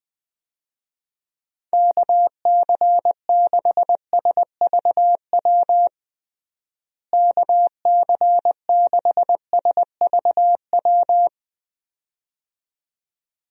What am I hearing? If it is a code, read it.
Morse code, heard as a keyed tone, about 20 wpm, 700 Hz. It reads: KC6SVW KC6SVW
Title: USA Call Signs - 6 Characters - Repeated 2x - 20wpm